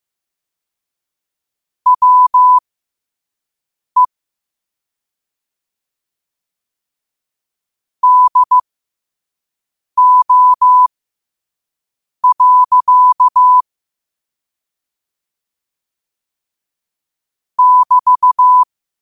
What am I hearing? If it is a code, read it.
Morse code, heard as a keyed tone, about 15 wpm, spaced as for 5 wpm, 1000 Hz. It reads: WE DO. <BT>
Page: New quotes every day in morse code at 5 Words per minute.